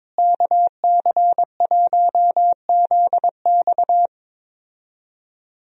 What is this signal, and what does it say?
Morse code, heard as a keyed tone, about 21 wpm, 700 Hz. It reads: KC1ZX
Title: USA Call Signs - 5 Characters - Repeated 2x - 22wpm